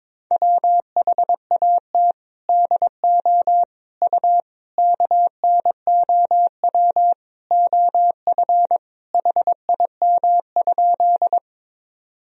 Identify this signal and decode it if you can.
Morse code, heard as a keyed tone, about 22 wpm, 700 Hz. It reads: WHAT DO U KNOW OF HIM?
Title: Sentences from Top 100 Words and Abbreviations 22wpm